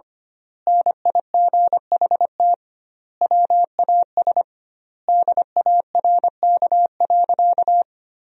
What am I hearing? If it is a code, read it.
Morse code, heard as a keyed tone, about 25 wpm, 700 Hz. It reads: NIGHT WAS DARK.
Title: 2x Word-Spacing - Sentences from Top 500 Words - 25wpm